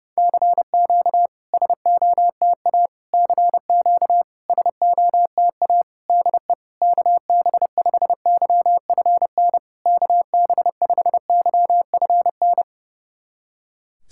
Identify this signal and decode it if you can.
Morse code, heard as a keyed tone, about 32 wpm, 690 Hz. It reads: CQ SOTA CQ SOTA DE KB5YFN KB5YFN
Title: SOTA QSO Practice 30wpm